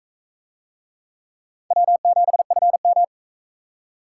Morse code, heard as a keyed tone, about 44 wpm, 700 Hz. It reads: W7FM